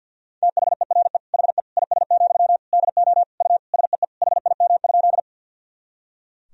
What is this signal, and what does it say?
Morse code, heard as a keyed tone, about 52 wpm, 700 Hz. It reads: THERE HE IS, DO U SEE HIM?
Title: Sentences from Top 100 Words and Abbreviations 50wpm